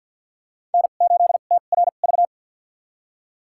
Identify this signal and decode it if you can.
Morse code, heard as a keyed tone, about 52 wpm, spaced as for 36 wpm, 700 Hz. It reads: N8TRV
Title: USA Call Signs - 5 Characters - 2x Character Spacing - 50wpm